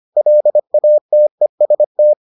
Morse code, heard as a keyed tone, about 25 wpm, 600 Hz. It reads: LATEST